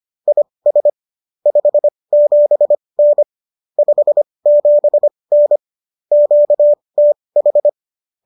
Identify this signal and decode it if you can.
Morse code, heard as a keyed tone, about 25 wpm, spaced as for 20 wpm, 590 Hz. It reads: IS 57N 57N QTH
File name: qso219 20